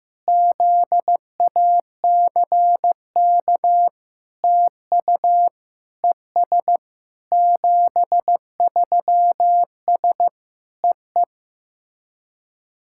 Morse code, heard as a keyed tone, about 15 wpm, 700 Hz. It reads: ZACK TU ES 73S EE
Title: SOTA QSO Practice 15wpm